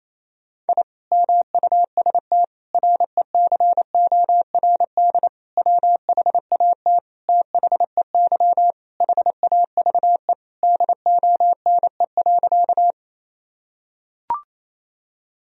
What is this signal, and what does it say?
Morse code, heard as a keyed tone, about 28 wpm, 700 Hz. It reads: I MUST RECORD WHAT THEY HAVE DONE.
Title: Sentences from Top 1000 Words - Review Part 3 - 28wpm